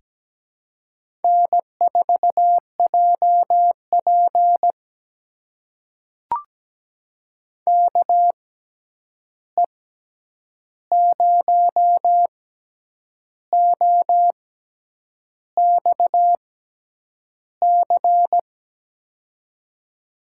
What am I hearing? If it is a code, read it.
Morse code, heard as a keyed tone, about 17 wpm, spaced as for 8 wpm, 700 Hz. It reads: N4JP KE0OXC
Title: USA Call Signs - 6x Character Spacing - 17wpm